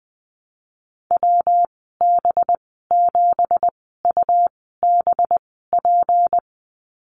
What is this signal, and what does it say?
Morse code, heard as a keyed tone, about 20 wpm, spaced as for 15 wpm, 700 Hz. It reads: WB7UBP
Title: 186 Lesson 30g - Letter b - Call Signs 20wpm